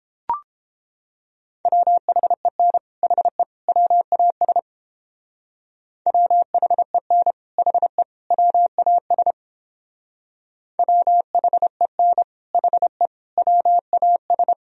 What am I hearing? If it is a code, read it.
Morse code, heard as a keyed tone, about 31 wpm, 700 Hz. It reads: WHEN HE WAS WHEN HE WAS WHEN HE WAS